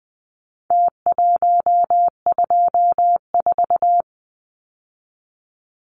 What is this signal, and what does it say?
Morse code, heard as a keyed tone, about 20 wpm, 700 Hz. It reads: T124
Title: Numbers - Contest Style - 4-Digit 20wpm